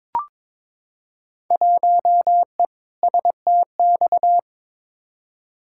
Morse code, heard as a keyed tone, about 22 wpm, 700 Hz. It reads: 1E STX
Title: Field Day Exchanges - 22wpm